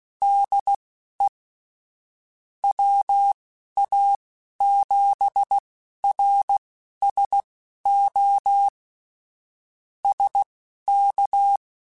Morse code, heard as a keyed tone, about 16 wpm, spaced as for 11 wpm, 800 Hz. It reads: DE WA7RSO SK